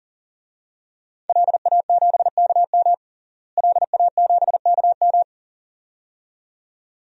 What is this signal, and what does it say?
Morse code, heard as a keyed tone, about 40 wpm, 700 Hz. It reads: LA7KM LA7KM
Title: DXCC-100 Call Signs - 5 Characters - Repeated 2x - 40wpm